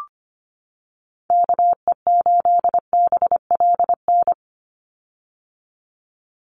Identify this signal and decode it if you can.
Morse code, heard as a keyed tone, about 25 wpm, 700 Hz. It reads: KE8BLN
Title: USA Call Signs 25wpm